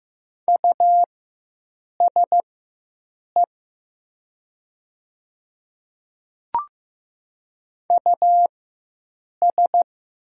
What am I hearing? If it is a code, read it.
Morse code, heard as a keyed tone, about 15 wpm, spaced as for 7 wpm, 700 Hz. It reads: USE US
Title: 4x Character Spacing - Top 300 Words - 15wpm